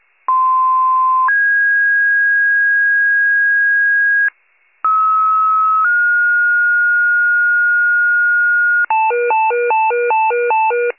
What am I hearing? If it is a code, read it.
Two-tone page, heard as a one-second tone, then a three-second tone, about 1040 then 1650 Hz, about 1290 then 1430 Hz, with noise. It is heard from 0.3 to 8.9 s.